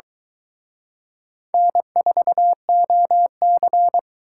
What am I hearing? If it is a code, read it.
Morse code, heard as a keyed tone, about 23 wpm, 700 Hz. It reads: N4OC